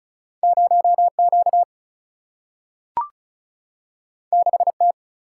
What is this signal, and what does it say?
Morse code, heard as a keyed tone, about 36 wpm, 700 Hz. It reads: 0Q 6T
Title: Single-Letter Single-Number 35wpm